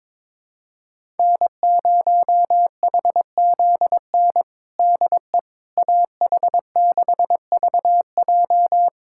Morse code, heard as a keyed tone, about 22 wpm, 700 Hz. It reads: N0HZN DE AH6VJ
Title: USA Callsign DE Callsign - 5 Characters - 22wpm